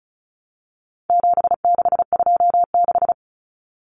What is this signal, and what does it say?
Morse code, heard as a keyed tone, about 34 wpm, 700 Hz. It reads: 7626
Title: Numbers - 4-Digit 35wpm